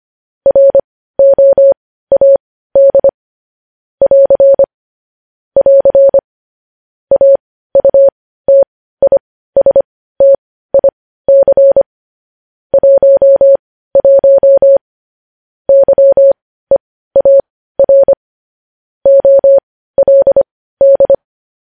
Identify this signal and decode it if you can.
Morse code, heard as a keyed tone, about 25 wpm, spaced as for 15 wpm, 550 Hz. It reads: ROAD <AR> <AR> AUTISTIC 11 YEAR OLD
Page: news-headlines-in-morse-code-at-15-wpm